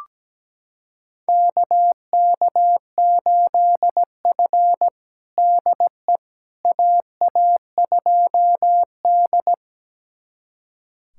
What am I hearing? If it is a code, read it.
Morse code, heard as a keyed tone, about 17 wpm, 700 Hz. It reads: KK8F DE AA2D
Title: USA Callsign DE Callsign - 4 Characters - 17wpm